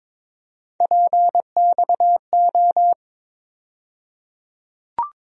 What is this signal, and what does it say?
Morse code, heard as a keyed tone, about 22 wpm, 700 Hz. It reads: PXO
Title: USA Call Sign Postfixes - 3 Letters 22wpm